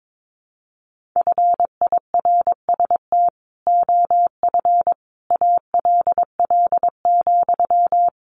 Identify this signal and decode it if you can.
Morse code, heard as a keyed tone, about 22 wpm, 700 Hz. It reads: FIRST OF ALL,
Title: Sentences from Top 1500 Words 22wpm